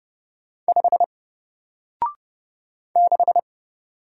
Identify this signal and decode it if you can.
Morse code, heard as a keyed tone, about 29 wpm, 700 Hz. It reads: 5 6
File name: Single-Letter-Number-30wpm